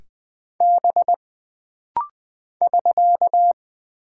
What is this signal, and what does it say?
Morse code, heard as a keyed tone, about 20 wpm, 700 Hz. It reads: B <SK>
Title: Single Character (Letter:Number:Punctuation:Prosign) - Rapid-Fire - 20wpm